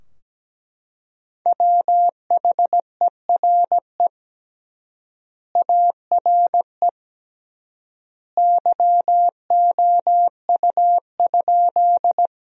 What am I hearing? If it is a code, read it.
Morse code, heard as a keyed tone, about 17 wpm, 700 Hz. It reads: WHERE ARE YOU?
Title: 3x Word-Spacing - Sentences from Top 200 Words - 17wpm